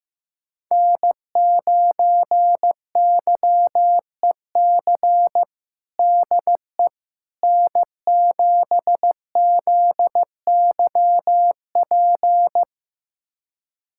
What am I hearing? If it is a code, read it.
Morse code, heard as a keyed tone, about 15 wpm, 700 Hz. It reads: N9YEC DE N7ZYP
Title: USA Callsign DE Callsign - 5 Characters - 15wpm